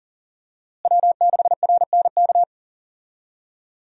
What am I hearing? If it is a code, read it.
Morse code, heard as a keyed tone, about 40 wpm, 700 Hz. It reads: W6RNK
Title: USA Call Signs - 5 Characters - Repeated 2x - 40wpm